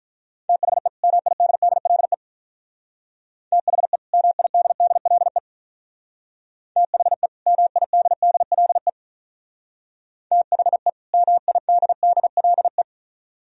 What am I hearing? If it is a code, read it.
Morse code, heard as a keyed tone, about 42 wpm, 700 Hz. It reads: THE MIDDLE THE MIDDLE THE MIDDLE THE MIDDLE
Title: Speed-Racing - Sets of 2 Words 35wpm